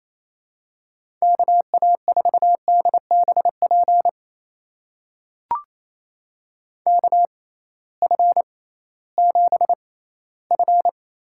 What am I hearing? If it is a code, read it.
Morse code, heard as a keyed tone, about 28 wpm, spaced as for 16 wpm, 700 Hz. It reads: KA4DBP KF7F
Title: USA Call Signs - 6 Characters - 6x Character Spacing - 28wpm